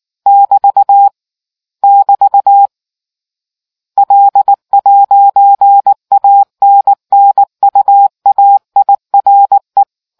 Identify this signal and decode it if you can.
Morse code, heard as a keyed tone, about 19 wpm, 800 Hz. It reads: <BT> <BT> L'ANNUAIRE